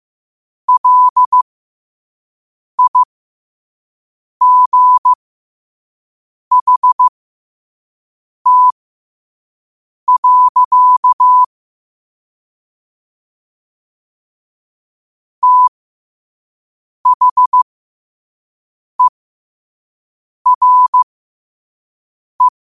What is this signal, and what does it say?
Morse code, heard as a keyed tone, about 15 wpm, spaced as for 5 wpm, 1000 Hz. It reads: LIGHT. THERE
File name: qotdfiles / 5wpm